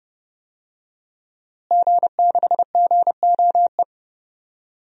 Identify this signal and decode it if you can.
Morse code, heard as a keyed tone, about 30 wpm, 700 Hz. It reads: G6GOE
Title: DXCC-100 Call Signs - 5 Characters - Repeated 2x - 30wpm